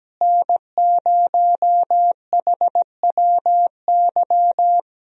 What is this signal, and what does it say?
Morse code, heard as a keyed tone, about 17 wpm, 700 Hz. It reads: N0HWY
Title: USA Call Signs - 5 Characters - Repeated 3x - 17wpm